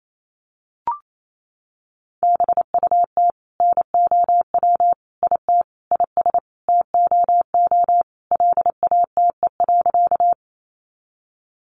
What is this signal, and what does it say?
Morse code, heard as a keyed tone, about 28 wpm, 700 Hz. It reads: BUT NOW IT IS TOO LATE.
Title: Sentences from Top 300 Words 28wpm